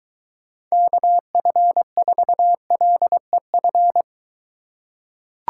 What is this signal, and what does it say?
Morse code, heard as a keyed tone, about 23 wpm, 700 Hz. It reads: KF4LEF